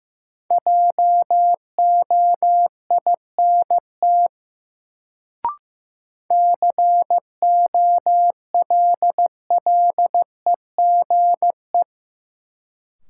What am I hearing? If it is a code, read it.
Morse code, heard as a keyed tone, about 15 wpm, 700 Hz. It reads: JOINT COLLEGE
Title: Sentences from Top 1300 Words - New Word Drill Alt - 15wpm